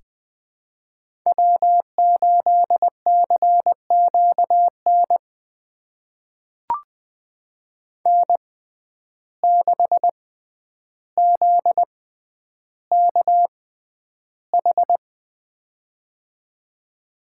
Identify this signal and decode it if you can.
Morse code, heard as a keyed tone, about 20 wpm, spaced as for 10 wpm, 700 Hz. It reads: W8CQN N6ZKH
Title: USA Call Signs - 5 Characters - 6x Character Spacing - 20wpm